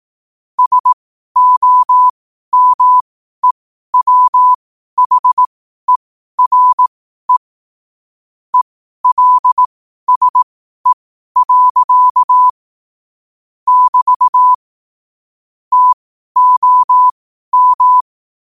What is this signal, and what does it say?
Morse code, heard as a keyed tone, about 18 wpm, spaced as for 12 wpm, 1000 Hz. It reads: SOMEWHERE ELSE. <BT> TOM
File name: qotdfiles / 12wpm